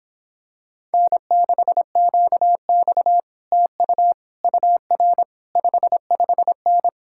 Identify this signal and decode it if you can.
Morse code, heard as a keyed tone, about 26 wpm, 700 Hz. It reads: N6QX TU UR 55N